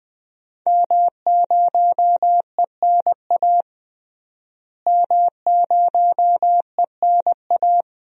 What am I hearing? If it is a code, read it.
Morse code, heard as a keyed tone, about 20 wpm, 700 Hz. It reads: M0ENA M0ENA
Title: DXCC-100 Call Signs - 5 Characters - Repeated 3x - 20wpm